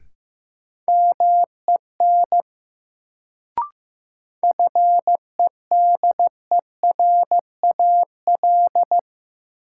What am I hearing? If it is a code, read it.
Morse code, heard as a keyed tone, about 15 wpm, 700 Hz. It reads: MEN FEDERAL
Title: Sets of 3 Words - New Word Drill - 15wpm